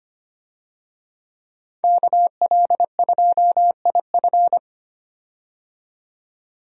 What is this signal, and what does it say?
Morse code, heard as a keyed tone, about 25 wpm, 700 Hz. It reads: KL2IF
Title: USA Call Signs - 5 Characters - 25wpm